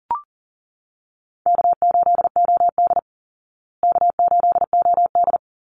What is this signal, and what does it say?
Morse code, heard as a keyed tone, about 37 wpm, 700 Hz. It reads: K8OD K8OD
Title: Easy Call Signs Repeated 3x - 40wpm